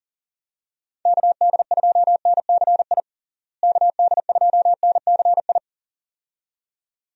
Morse code, heard as a keyed tone, about 40 wpm, 700 Hz. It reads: KD2NCI KD2NCI
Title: USA Call Signs - 6 Characters - Repeated 2x - 40wpm